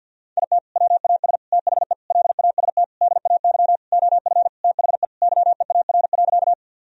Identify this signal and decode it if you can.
Morse code, heard as a keyed tone, about 48 wpm, 700 Hz. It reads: IT WAS THE LAST DAY OF THE YEAR.
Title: Sentences from Top 1000 Words - Review Part 1 - 50wpm